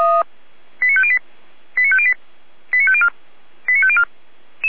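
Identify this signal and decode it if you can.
Five-tone selective call (ZVEI), heard as quick runs of five tones, tones of about 70 ms. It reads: 89508 89508 89504 89504
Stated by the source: ZVEI five-tone call received on an SDR, from the feuerwehr-orlamuende fire brigade audio archive.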